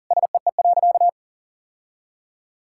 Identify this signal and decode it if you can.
Morse code, heard as a keyed tone, about 40 wpm, 700 Hz. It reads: SEE.
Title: Sentences from Top 1000 Words - Review Part 5 - 40wpm